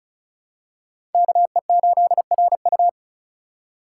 Morse code, heard as a keyed tone, about 36 wpm, 700 Hz. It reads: KE8RU
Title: K1USN SST QSO Practice - 35wpm